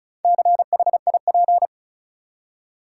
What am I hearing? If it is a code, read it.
Morse code, heard as a keyed tone, about 34 wpm, 700 Hz. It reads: CHIP
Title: Four Letter Words - 35wpm